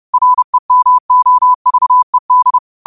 Morse code, heard as a keyed tone, about 30 wpm, 1000 Hz. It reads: REMOVED
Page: New quotes every day in morse code at 30 Words per minute.